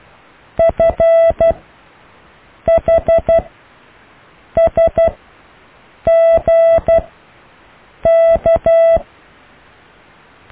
Morse code, heard as a keyed tone, about 12 wpm, spaced as for 6 wpm, 660 Hz. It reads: FHSGK
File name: cw / Alle letters